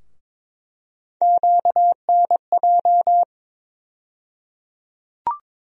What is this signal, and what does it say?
Morse code, heard as a keyed tone, about 22 wpm, 700 Hz. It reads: QNJ
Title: USA Call Sign Postfixes - 3 Letters 22wpm